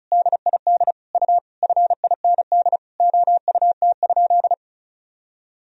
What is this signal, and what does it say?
Morse code, heard as a keyed tone, about 34 wpm, 700 Hz. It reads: DID U FIND OUT?
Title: Sentences from Top 100 Words and Abbreviations 35wpm